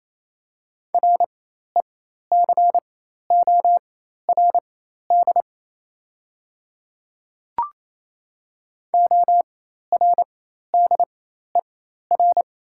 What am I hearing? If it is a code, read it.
Morse code, heard as a keyed tone, about 28 wpm, spaced as for 13 wpm, 700 Hz. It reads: RECORD ORDER
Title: 4x Character Spacing - Top 500 Words - 28wpm